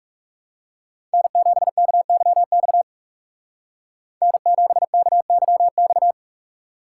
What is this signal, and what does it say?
Morse code, heard as a keyed tone, about 36 wpm, 700 Hz. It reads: N7KYX N7KYX
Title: Speed Racing - USA Call Signs 30wpm